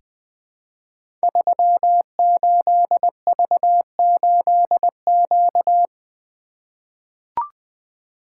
Morse code, heard as a keyed tone, about 20 wpm, 700 Hz. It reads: 38V8Q
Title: Code Groups - 5 Letter:Numbers - 20wpm